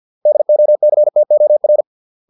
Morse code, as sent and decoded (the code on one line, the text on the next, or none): -.. --- -.-. - --- .-.
DOCTOR